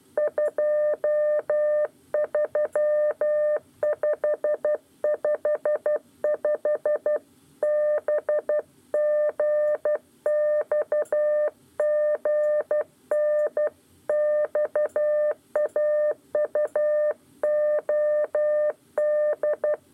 Morse code, heard as a keyed tone, about 12 wpm, 600 Hz. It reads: 23555 BGXGN XAUOD
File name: 8 lezione cw 1